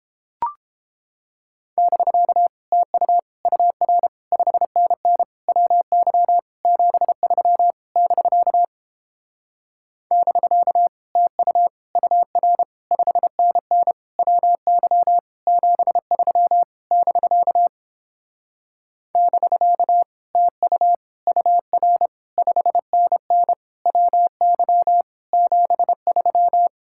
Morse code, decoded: <BK> TU UR 5NN WY 73 <BK> <BK> TU UR 5NN WY 73 <BK> <BK> TU UR 5NN WY 73

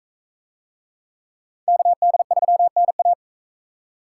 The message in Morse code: KD3NA

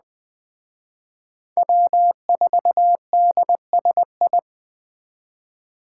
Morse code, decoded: W4DSI

